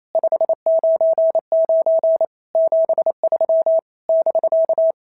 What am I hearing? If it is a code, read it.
Morse code, heard as a keyed tone, about 27 wpm, 650 Hz. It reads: 599 73 <BK>
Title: SOTA QSO Practice 28wpm